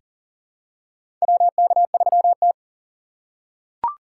Morse code, heard as a keyed tone, about 38 wpm, 700 Hz. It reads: WK3T
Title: USA Call Signs - 4 Characters - 2x Character Spacing - 40wpm